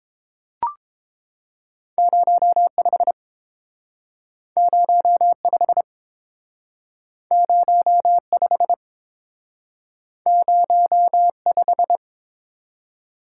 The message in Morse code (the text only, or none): 05 05 05 05